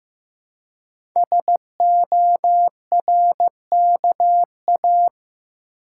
Morse code, heard as a keyed tone, about 15 wpm, 700 Hz. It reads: SORKA
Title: Code Groups - 5 Letters - 15wpm